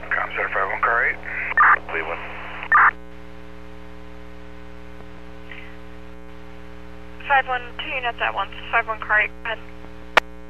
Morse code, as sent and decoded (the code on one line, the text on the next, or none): none